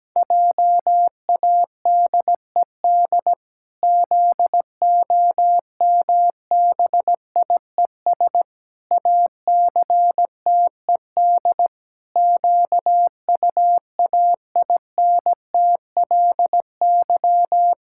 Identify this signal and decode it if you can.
Morse code, heard as a keyed tone, about 17 wpm, 700 Hz. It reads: JADED ZOMBIES ACTED QUAINTLY
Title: Pangram Challenge v2 - 17wpm